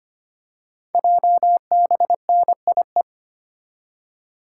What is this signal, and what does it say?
Morse code, heard as a keyed tone, about 25 wpm, 700 Hz. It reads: JBNIE
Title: Code Groups - 5 Letters - 25wpm